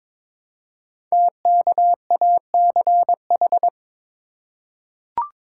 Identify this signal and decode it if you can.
Morse code, heard as a keyed tone, about 22 wpm, 700 Hz. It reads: TKACH
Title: Code Groups - 5 Letters - 22wpm